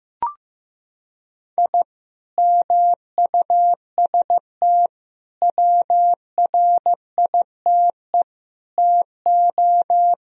Morse code, heard as a keyed tone, about 15 wpm, 700 Hz. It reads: I MUST WRITE TO
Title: Sentences from Top 1000 Words - Review Part 1 - 15wpm